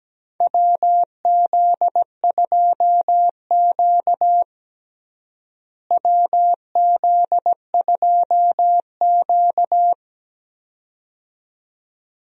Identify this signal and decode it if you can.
Morse code, heard as a keyed tone, about 17 wpm, 700 Hz. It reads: WZ2Q WZ2Q
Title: USA Call Signs - 4 Characters - Repeated 2x - 17wpm